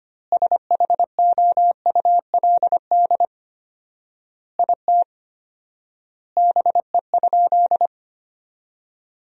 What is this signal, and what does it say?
Morse code, heard as a keyed tone, about 25 wpm, 700 Hz. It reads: SHOULD IT BE?
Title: 4x Word-Spacing - Sentences from Top 300 Words - 25wpm